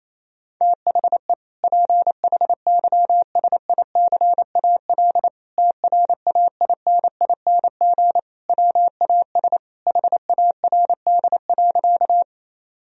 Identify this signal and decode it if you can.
Morse code, heard as a keyed tone, about 28 wpm, 700 Hz. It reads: THE PHYSICAL TRAINING WAS HARD.
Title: Sentences from Top 1000 Words - Review Part 4 - 28wpm